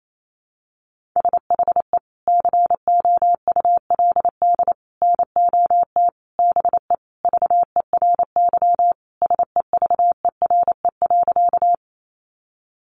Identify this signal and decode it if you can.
Morse code, heard as a keyed tone, about 28 wpm, 700 Hz. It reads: SHE COULD NOT BE VERY SEVERE.
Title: Sentences from Top 1400 Words 28wpm